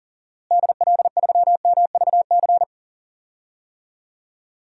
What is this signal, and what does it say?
Morse code, heard as a keyed tone, about 37 wpm, 700 Hz. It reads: DL3MVC
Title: DXCC-100 Call Signs - 6 Characters - Repeated 3x - 40wpm